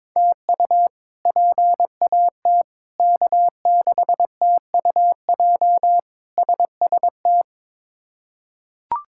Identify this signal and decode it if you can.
Morse code, heard as a keyed tone, about 22 wpm, 700 Hz. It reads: TU PAT K6TUJ SST
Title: K1USN SST QSO Practice - 22wpm